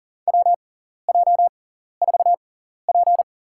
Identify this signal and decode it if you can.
Morse code, heard as a keyed tone, about 40 wpm, spaced as for 14 wpm, 700 Hz. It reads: WJ4P
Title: USA Call Signs - 6x Character Spacing - 40wpm